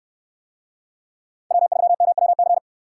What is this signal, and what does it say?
Morse code, heard as a keyed tone, about 67 wpm, 700 Hz. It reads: W3MWC